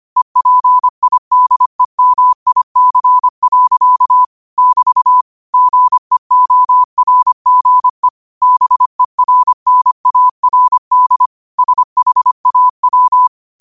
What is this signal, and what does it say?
Morse code, heard as a keyed tone, about 25 wpm, 1000 Hz. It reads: EPIDEMIC. <BT> GEORGE BERNARD SHAW
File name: qotdfiles / 25wpm